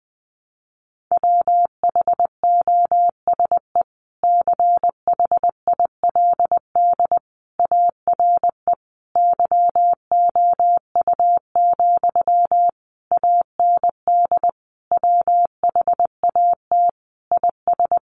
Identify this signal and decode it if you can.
Morse code, heard as a keyed tone, about 20 wpm, 700 Hz. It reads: WHOSE CHILD ARE YOU, AND WHAT IS
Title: Sentences from Top 1200 Words 20wpm